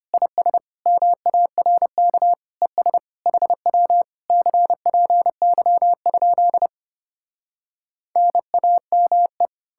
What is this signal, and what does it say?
Morse code, heard as a keyed tone, about 33 wpm, 700 Hz. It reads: IS MARK ES HW CPY? NAME